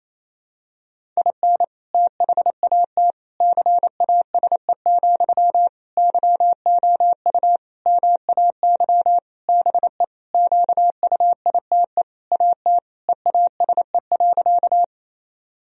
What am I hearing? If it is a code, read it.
Morse code, heard as a keyed tone, about 28 wpm, 700 Hz. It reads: IN THAT CASE, YOU MAY BE QUITE AT EASE.
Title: Sentences from Top 1000 Words - Review Part 1 - 28wpm